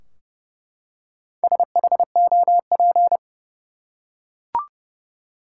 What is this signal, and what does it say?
Morse code, heard as a keyed tone, about 31 wpm, 700 Hz. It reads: SHOP
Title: Four Letter Words - 30wpm